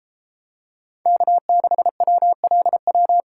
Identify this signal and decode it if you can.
Morse code, heard as a keyed tone, about 33 wpm, 700 Hz. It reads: K6WLW